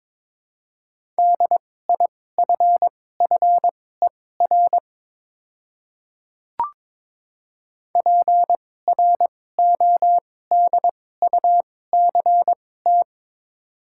Morse code, heard as a keyed tone, about 22 wpm, spaced as for 16 wpm, 690 Hz. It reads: DIFFER PRODUCT
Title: 2x Character Spacing - Top 500 Words - 22wpm